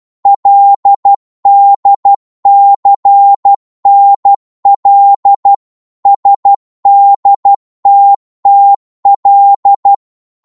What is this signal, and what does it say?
Morse code, heard as a keyed tone, about 12 wpm, 800 Hz. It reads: LDCNL SDTTL
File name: Morse / ltr-12wpm